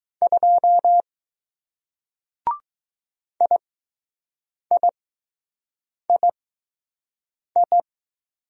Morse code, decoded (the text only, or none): 2 IIII